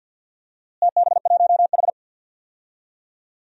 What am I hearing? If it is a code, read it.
Morse code, heard as a keyed tone, about 52 wpm, 700 Hz. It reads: TB1H
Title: DXCC-100 Call Signs - 4 Characters - Repeated 3x - 50wpm